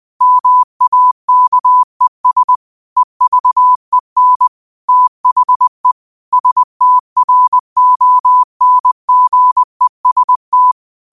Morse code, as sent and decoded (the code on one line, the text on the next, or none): -- .- -.- . ... / . ...- . -. / - .... / ... - .-. --- -. --. ... -
MAKES EVEN THE STRONGEST